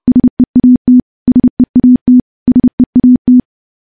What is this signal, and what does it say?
Morse code, heard as a keyed tone, about 31 wpm, 250 Hz. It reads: SEAT SEAT SEAT
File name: morse / morse-training-30WPM